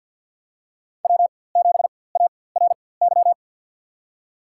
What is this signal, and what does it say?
Morse code, heard as a keyed tone, about 48 wpm, spaced as for 23 wpm, 700 Hz. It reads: W7ARY
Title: USA Call Signs - 4x Character Spacing - 50wpm